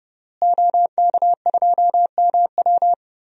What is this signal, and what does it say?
Morse code, heard as a keyed tone, about 30 wpm, 700 Hz. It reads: OK2MW
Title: DXCC-100 Call Signs - 5 Characters - Repeated 3x - 30wpm